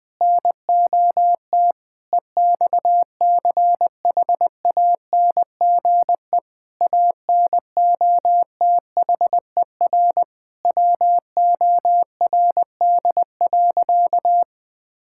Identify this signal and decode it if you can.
Morse code, heard as a keyed tone, about 20 wpm, 700 Hz. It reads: NOT EXCHANGE ANOTHER WORD.
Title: Sentences from Top 1000 Words - Review Part 6 - 20wpm